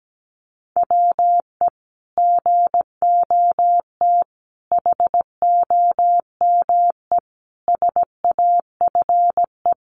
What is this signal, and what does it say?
Morse code, heard as a keyed tone, about 17 wpm, 700 Hz. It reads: WE GOT HOME SAFE
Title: Sentences from Top 900 Words 17wpm